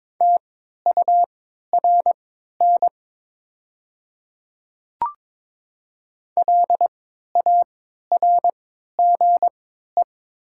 Morse code, heard as a keyed tone, about 22 wpm, spaced as for 13 wpm, 700 Hz. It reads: TURN LARGE
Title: 3x Character Spacing - Top 200 Words - 22wpm